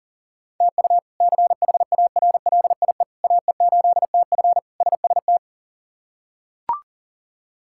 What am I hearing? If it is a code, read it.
Morse code, heard as a keyed tone, about 40 wpm, 700 Hz. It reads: TU CHARLIE AE8TF SST
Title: K1USN SST QSO Practice - 40wpm